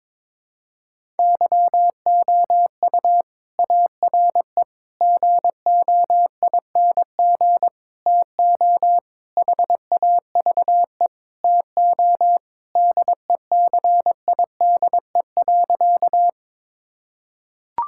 Morse code, heard as a keyed tone, about 22 wpm, 700 Hz. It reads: YOU ARE GOING TO HAVE TO DECIDE.
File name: practice / 22wpm / Sentences with 4-Word Phrases - 22wpm